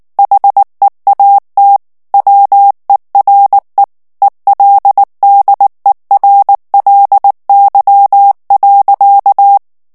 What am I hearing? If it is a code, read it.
Morse code, heard as a keyed tone, about 19 wpm, 800 Hz. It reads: HEAT WERE ELDERLY.
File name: morse mp3 / ANGLAIS texte clair 2 vitesse 18 mots minute